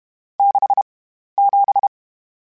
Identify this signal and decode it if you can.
Morse code, heard as a keyed tone, about 31 wpm, 800 Hz. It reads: 6 7
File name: Misc / Misc-mp3s / Numb @30